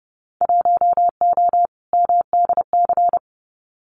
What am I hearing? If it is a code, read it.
Morse code, heard as a keyed tone, about 31 wpm, 700 Hz. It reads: 1O MDC
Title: Winter Field Day Exchanges 30wpm